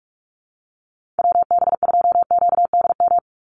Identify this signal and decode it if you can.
Morse code, heard as a keyed tone, about 44 wpm, 690 Hz. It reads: WB2QDM
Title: USA Call Signs 45wpm